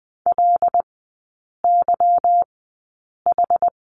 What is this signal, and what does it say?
Morse code, heard as a keyed tone, about 20 wpm, spaced as for 8 wpm, 690 Hz. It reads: LYH
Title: 191 Lesson 31e - Letter v - Three Characters 20wpm